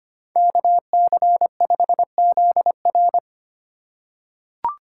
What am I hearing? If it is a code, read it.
Morse code, heard as a keyed tone, about 25 wpm, 700 Hz. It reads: KC5ZR
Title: USA Call Signs - 5 Characters - 4x Character Spacing - 25wpm